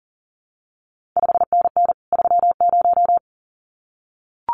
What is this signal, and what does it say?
Morse code, heard as a keyed tone, about 40 wpm, 700 Hz. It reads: 5NN 30
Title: CQ WW DX Contest Exchange Encore - 40wpm